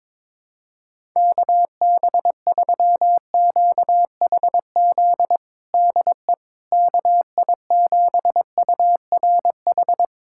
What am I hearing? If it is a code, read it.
Morse code, heard as a keyed tone, about 21 wpm, 700 Hz. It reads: KB3QHZ DE KI7URH
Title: USA Callsign DE Callsign - 6 Characters - 22wpm